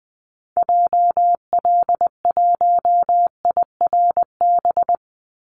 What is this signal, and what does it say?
Morse code, heard as a keyed tone, about 20 wpm, 700 Hz. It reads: JL1IRB
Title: Call Signs Repeated 3x - 20wpm